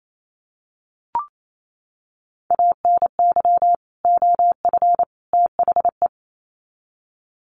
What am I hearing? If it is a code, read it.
Morse code, heard as a keyed tone, about 28 wpm, 700 Hz. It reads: ANY OF THE
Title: Sets of 3 Words 28wpm